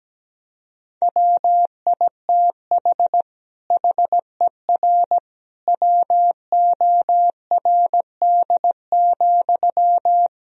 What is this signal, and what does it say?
Morse code, heard as a keyed tone, about 17 wpm, 700 Hz. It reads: WITH HER WORD,
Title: Sentences from Top 1000 Words - Review Part 2 - 17wpm